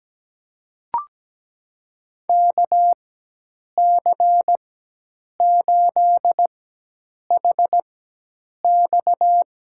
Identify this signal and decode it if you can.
Morse code, heard as a keyed tone, about 17 wpm, spaced as for 8 wpm, 700 Hz. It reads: KC8HX
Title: USA Call Signs - 4x Character Spacing - 17wpm